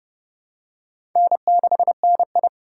Morse code, heard as a keyed tone, about 28 wpm, 690 Hz. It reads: N6NI
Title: USA Call Signs - 4 Characters - 2x Character Spacing - 30wpm